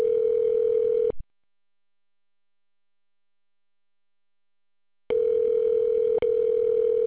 Ringback tone (North America), 440 + 480 Hz.